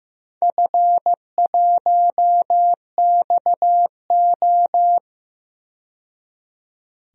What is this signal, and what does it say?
Morse code, heard as a keyed tone, about 15 wpm, 700 Hz. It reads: F1XO
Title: DXCC-100 Call Signs - 4 Characters - 15wpm